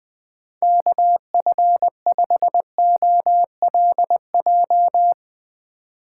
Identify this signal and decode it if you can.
Morse code, heard as a keyed tone, about 20 wpm, 700 Hz. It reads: KF5OLJ
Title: USA Call Signs - 6 Characters - 4x Character Spacing - 20wpm